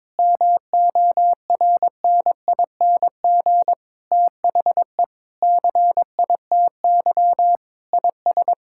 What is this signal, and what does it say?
Morse code, heard as a keyed tone, about 21 wpm, 700 Hz. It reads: MORNING THE CITY IS
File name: Sentences with 7-Word Phrases - 22wpm